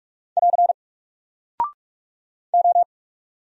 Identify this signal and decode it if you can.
Morse code, heard as a keyed tone, about 44 wpm, 700 Hz. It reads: <AR> O
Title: Single Character (Letter:Number:Punctuation:Prosign) Alt 45wpm